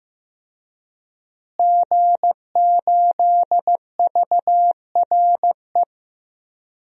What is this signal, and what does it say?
Morse code, heard as a keyed tone, about 15 wpm, 700 Hz. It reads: G8VRE
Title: DXCC-100 Call Signs - 5 Characters - Repeated 3x - 15wpm